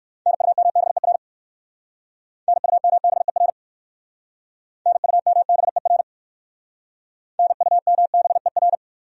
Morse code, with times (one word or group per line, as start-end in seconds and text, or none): 0.3-1.2 NUMBER
2.5-3.5 NUMBER
4.9-6.0 NUMBER
7.4-8.8 NUMBER